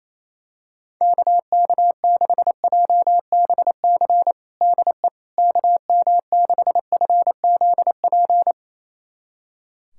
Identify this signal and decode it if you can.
Morse code, heard as a keyed tone, about 28 wpm, 700 Hz. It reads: KK6JBC DE KM6FZP
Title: USA Callsign DE Callsign - 6 Characters - 28wpm